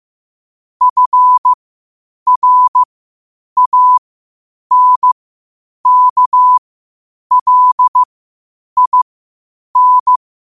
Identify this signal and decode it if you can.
Morse code, heard as a keyed tone, about 15 wpm, spaced as for 8 wpm, 1000 Hz. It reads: FRANKLIN
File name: qotdfiles / 8wpm